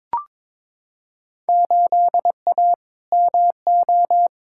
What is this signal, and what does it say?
Morse code, heard as a keyed tone, about 22 wpm, 700 Hz. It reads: 8A MO
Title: Field Day Exchanges Encore - 22wpm